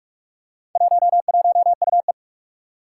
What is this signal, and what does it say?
Morse code, heard as a keyed tone, about 44 wpm, 700 Hz. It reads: 11UE
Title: Four Challenging Letter-Number Combinations 45wpm